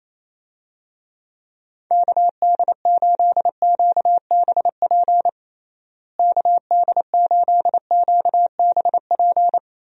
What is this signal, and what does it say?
Morse code, heard as a keyed tone, about 28 wpm, 700 Hz. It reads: KD8QBP KD8QBP
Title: USA Call Signs - 6 Characters - Repeated 3x - 28wpm